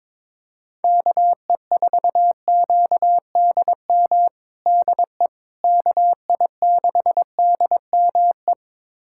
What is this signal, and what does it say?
Morse code, heard as a keyed tone, about 21 wpm, 700 Hz. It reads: KE4QDM DE KI6DME
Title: USA Callsign DE Callsign - 6 Characters - 22wpm